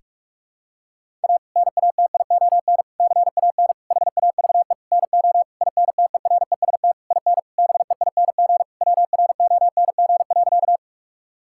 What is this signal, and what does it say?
Morse code, heard as a keyed tone, about 41 wpm, 700 Hz. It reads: A NATION CAN HAVE NO INTEREST IN BEING WRONG.
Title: Sentences from Top 1000 Words - Review Part 6 - 45wpm